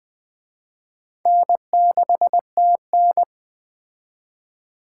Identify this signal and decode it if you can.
Morse code, heard as a keyed tone, about 20 wpm, 700 Hz. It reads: N6TN